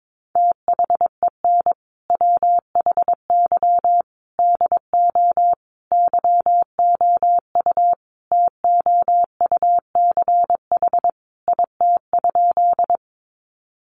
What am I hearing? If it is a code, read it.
Morse code, heard as a keyed tone, about 22 wpm, 700 Hz. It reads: THEN WHY DO YOU TOUCH IT?
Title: Sentences from Top 1200 Words 22wpm